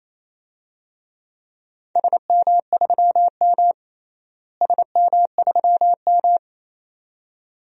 Morse code, heard as a keyed tone, about 28 wpm, 700 Hz. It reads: SM3M SM3M
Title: DXCC-100 Call Signs - 4 Characters - Repeated 2x - 28wpm